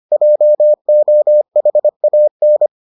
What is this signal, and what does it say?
Morse code, heard as a keyed tone, about 25 wpm, 600 Hz. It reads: JOHAN